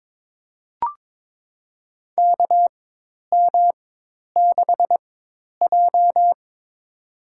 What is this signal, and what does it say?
Morse code, heard as a keyed tone, about 22 wpm, spaced as for 10 wpm, 700 Hz. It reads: KM6J
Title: USA Call Signs - 4x Character Spacing - 22wpm